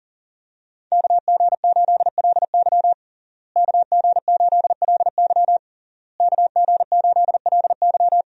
Morse code, heard as a keyed tone, about 38 wpm, 700 Hz. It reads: KG8LY KG8LY KG8LY
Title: USA Call Signs - 5 Characters - Repeated 3x - 40wpm